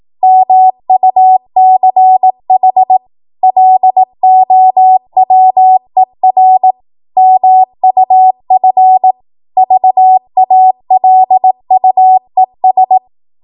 Morse code, heard as a keyed tone, about 18 wpm, 750 Hz. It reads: MUCH LOWER MUF VALUES
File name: Morse / 18WPM